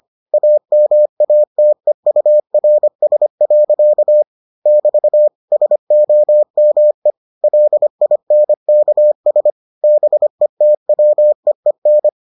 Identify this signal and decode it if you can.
Morse code, heard as a keyed tone, about 25 wpm, 600 Hz. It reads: AMATEURS. <BT> SOME LINKS BETWEEN